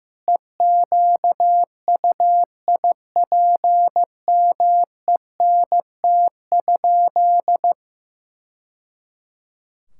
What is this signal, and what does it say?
Morse code, heard as a keyed tone, about 16 wpm, 700 Hz. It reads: EQUIPMENT?
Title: Sentences from Top 1000 Words - Review Part 6 - 15wpm